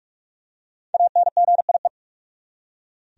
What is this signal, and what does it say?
Morse code, heard as a keyed tone, about 42 wpm, 700 Hz. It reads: ANGIE
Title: USA Common Names - Max 5 Letters - 45wpm